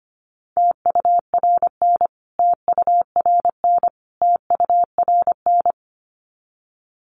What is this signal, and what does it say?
Morse code, heard as a keyed tone, about 25 wpm, 700 Hz. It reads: TURN TURN TURN
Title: Top 500 Words Repeated 25wpm